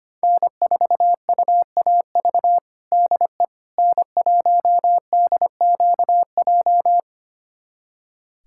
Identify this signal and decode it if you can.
Morse code, heard as a keyed tone, about 25 wpm, 700 Hz. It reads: N4UAV DE N1DQJ
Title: USA Callsign DE Callsign - 5 Characters - 25wpm